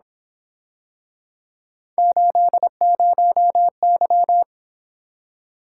Morse code, heard as a keyed tone, about 26 wpm, 700 Hz. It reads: 80Y
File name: Speed-Racing - Three Letter-Number Combinations - 22wpm